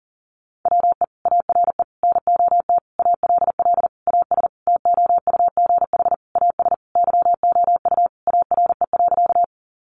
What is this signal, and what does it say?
Morse code, heard as a keyed tone, about 40 wpm, 700 Hz. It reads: WE ARE NOT ALL AS TOUGH AS YOU ARE.
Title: Sentences from Top 1200 Words 40wpm